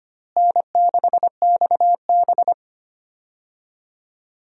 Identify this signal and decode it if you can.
Morse code, heard as a keyed tone, about 25 wpm, 700 Hz. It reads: N6XB
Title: USA Call Signs - 4 Characters - Repeated 2x - 25wpm